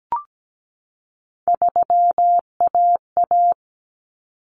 Morse code, heard as a keyed tone, about 17 wpm, 700 Hz. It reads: 3AA
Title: Common Cut Numbers - 3-Digit 17wpm